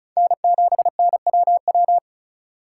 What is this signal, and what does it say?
Morse code, heard as a keyed tone, about 36 wpm, 700 Hz. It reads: N7NWW
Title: USA Call Signs - 5 Characters - 2x Character Spacing - 35wpm